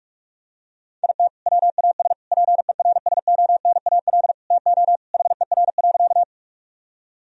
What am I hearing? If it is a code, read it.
Morse code, heard as a keyed tone, about 43 wpm, 700 Hz. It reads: IT WAS PERSONAL TO HER.
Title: Sentences from Top 1000 Words - Review Part 4 - 45wpm